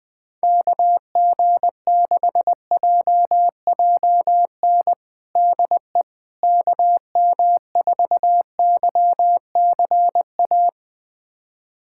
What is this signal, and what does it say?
Morse code, heard as a keyed tone, about 20 wpm, 700 Hz. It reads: KG6JJN DE KM4YCA
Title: USA Callsign DE Callsign - 6 Characters - 20wpm